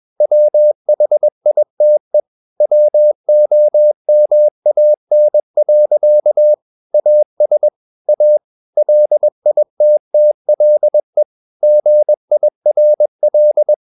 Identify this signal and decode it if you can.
Morse code, heard as a keyed tone, about 21 wpm, 600 Hz. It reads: WHITE WOMAN. AS A LITTLE GIRL
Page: CW med Gnister